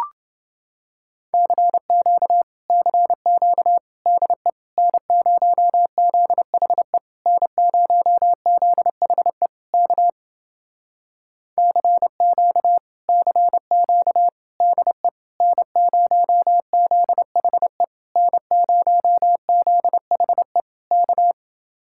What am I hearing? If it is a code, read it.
Morse code, heard as a keyed tone, about 31 wpm, 700 Hz. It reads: CQ CQ DE N0ZHE N0ZHE K CQ CQ DE N0ZHE N0ZHE K